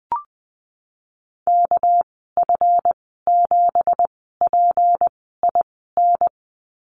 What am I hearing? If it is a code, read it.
Morse code, heard as a keyed tone, about 19 wpm, spaced as for 14 wpm, 700 Hz. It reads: KF7PIN
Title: 213 Lesson 34g - Number 8 - Call Signs 20wpm